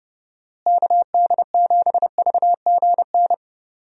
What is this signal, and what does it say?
Morse code, heard as a keyed tone, about 30 wpm, 700 Hz. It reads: KD7VGN
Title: USA Call Signs - 6 Characters - Repeated 2x - 30wpm